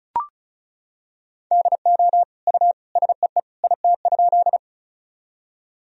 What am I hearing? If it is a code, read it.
Morse code, heard as a keyed tone, about 36 wpm, 700 Hz. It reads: DO U SEE IT?
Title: Sentences from Top 100 Words and Abbreviations 35wpm